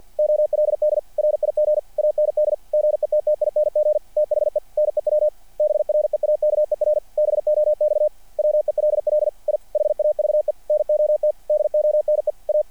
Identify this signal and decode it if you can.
Morse code, heard as a keyed tone, about 47 wpm, 600 Hz. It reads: OLD RIG AND GETTING THE NEW BREAKER BOX WELL I HAVE NOT DONE A
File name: archived / CRSnet